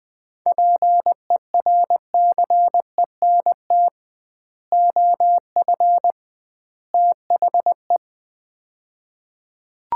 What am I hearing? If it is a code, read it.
Morse code, heard as a keyed tone, about 20 wpm, 700 Hz. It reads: PERCENT OF THE
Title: 2x Word-Spacing - Sets of 3 Words - 20wpm